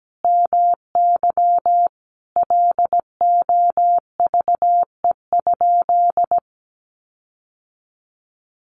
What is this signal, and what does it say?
Morse code, heard as a keyed tone, about 17 wpm, 700 Hz. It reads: MY LOVE?
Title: Sentences from Top 1500 Words 17wpm